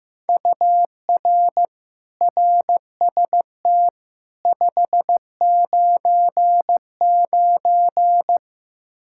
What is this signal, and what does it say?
Morse code, heard as a keyed tone, about 15 wpm, 700 Hz. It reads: UR RST 599
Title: Practice QSO - Exchange 1 - Beginning - 15wpm